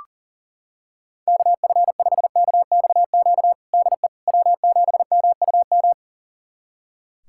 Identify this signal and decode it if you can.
Morse code, heard as a keyed tone, about 40 wpm, 700 Hz. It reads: KF5KXQ DE W7MUM
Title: USA Callsign DE Callsign 40wpm